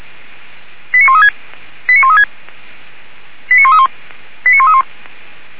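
Five-tone selective call (ZVEI), heard as quick runs of five tones, tones of about 70 ms. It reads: 89136 89136 89131 89131
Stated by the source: ZVEI five-tone call received by SDR, recorded by Feuerwehr Orlamünde.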